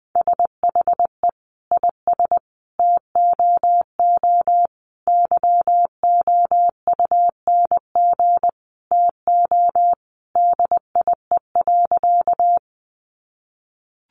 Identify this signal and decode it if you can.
Morse code, heard as a keyed tone, about 20 wpm, 700 Hz. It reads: SHE IS TOO YOUNG TO DIE.
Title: Sentences from Top 1300 Words 20wpm